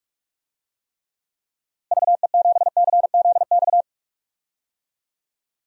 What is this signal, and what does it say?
Morse code, heard as a keyed tone, about 46 wpm, 700 Hz. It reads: VE7CZX
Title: DXCC-100 Call Signs - 6 Characters - Repeated 2x - 45wpm